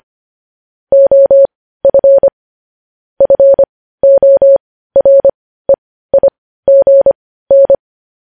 Morse code, heard as a keyed tone, about 25 wpm, spaced as for 15 wpm, 550 Hz. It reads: OF FOREIGN
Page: news headlines in morse code at 15 wpm